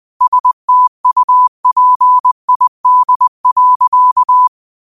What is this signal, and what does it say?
Morse code, heard as a keyed tone, about 20 wpm, 1000 Hz. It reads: STUPID.
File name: qotdfiles / 20wpm